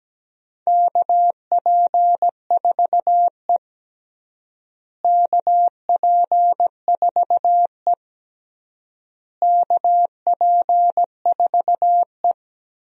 Morse code, heard as a keyed tone, about 17 wpm, 700 Hz. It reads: KP4E KP4E KP4E
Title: DXCC-100 Call Signs - 4 Characters - Repeated 3x - 17wpm